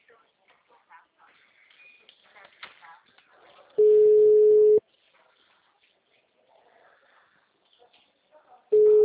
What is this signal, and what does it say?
Ringback tone (Europe, 425 Hz), with speech.